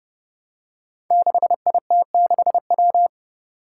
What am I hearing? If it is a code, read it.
Morse code, heard as a keyed tone, about 30 wpm, 700 Hz. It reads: 6IT6W
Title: Five Challenging Letter-Number Combinations 30wpm